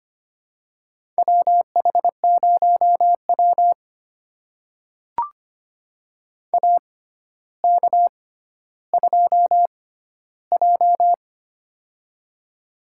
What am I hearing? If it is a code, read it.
Morse code, heard as a keyed tone, about 25 wpm, spaced as for 13 wpm, 700 Hz. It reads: WH0W AK2J
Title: USA Call Signs - 4 Characters - 6x Character Spacing - 25wpm